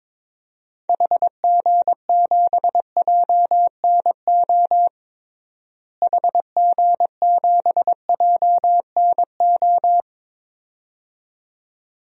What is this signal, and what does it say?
Morse code, heard as a keyed tone, about 22 wpm, 700 Hz. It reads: HG7JNO HG7JNO
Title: DXCC-100 Call Signs - 6 Characters - Repeated 2x - 22wpm